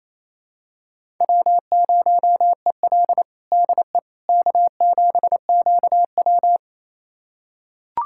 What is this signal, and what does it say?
Morse code, heard as a keyed tone, about 28 wpm, 700 Hz. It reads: W0EL DE K7QW
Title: USA Callsign DE Callsign - 4 Characters - 28wpm